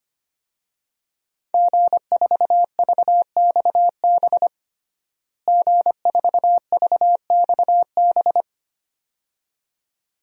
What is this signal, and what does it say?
Morse code, heard as a keyed tone, about 25 wpm, 700 Hz. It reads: G4VXB G4VXB
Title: DXCC-100 Call Signs - 5 Characters - Repeated 2x - 25wpm